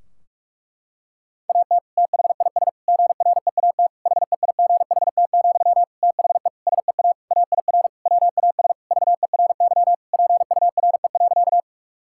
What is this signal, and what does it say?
Morse code, heard as a keyed tone, about 41 wpm, 700 Hz. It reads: AT THIS GREAT HEIGHT, THE SEA AIR WAS VERY PURE.
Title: Sentences from Top 1600 Words 45wpm